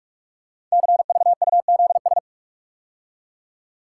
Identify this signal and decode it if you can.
Morse code, heard as a keyed tone, about 44 wpm, 700 Hz. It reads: CVUZS
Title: Code Groups - 5 Letters - 45wpm